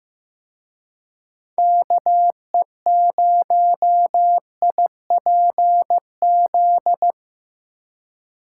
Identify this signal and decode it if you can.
Morse code, heard as a keyed tone, about 17 wpm, 700 Hz. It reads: KE0IPZ
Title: Speed Racing - USA 2x3 Call Signs 15wpm